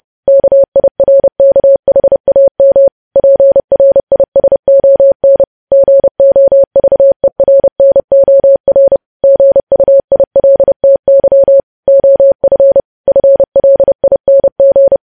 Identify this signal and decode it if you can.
Morse code, heard as a keyed tone, about 30 wpm, 550 Hz. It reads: KIRKHAM PRISON GOVERNOR GUILTY OF FLING
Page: news headlines in morse code at 30 wpm